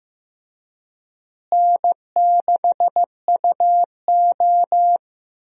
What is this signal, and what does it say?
Morse code, heard as a keyed tone, about 15 wpm, 700 Hz. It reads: N6UO
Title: USA Call Signs - 4 Characters - Repeated 3x - 15wpm